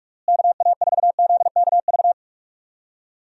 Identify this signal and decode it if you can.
Morse code, heard as a keyed tone, about 46 wpm, 700 Hz. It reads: KA4ZKV
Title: USA Call Signs - 6 Characters - Repeated 3x - 45wpm